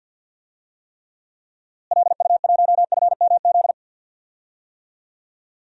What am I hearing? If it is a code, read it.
Morse code, heard as a keyed tone, about 51 wpm, 700 Hz. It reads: LU1FMZ
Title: DXCC-100 Call Signs - 6 Characters - Repeated 3x - 50wpm